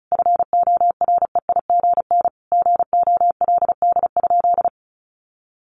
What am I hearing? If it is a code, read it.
Morse code, heard as a keyed tone, about 34 wpm, 700 Hz. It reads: FOREIGN GOLD?
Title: Sentences from Top 1500 Words 35wpm